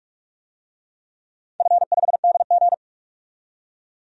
Morse code, heard as a keyed tone, about 43 wpm, 700 Hz. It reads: F5DG
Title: DXCC-100 Call Signs - 4 Characters - 45wpm